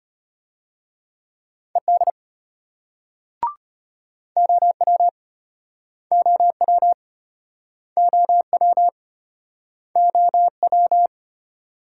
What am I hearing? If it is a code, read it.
Morse code, heard as a keyed tone, about 32 wpm, 700 Hz. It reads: ED OW OW OW OW